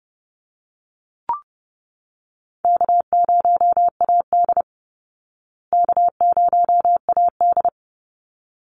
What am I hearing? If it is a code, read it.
Morse code, heard as a keyed tone, about 30 wpm, 700 Hz. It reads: K0AD K0AD
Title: Easy Call Signs Repeated 2x - 30wpm